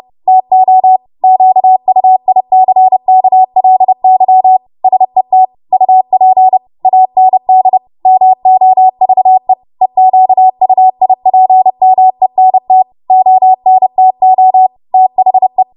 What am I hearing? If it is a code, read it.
Morse code, heard as a keyed tone, about 30 wpm, 750 Hz. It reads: TO QUICKLY SET UP AND MOVE EQUIPMENT ONTO THE
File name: Morse / 30WPM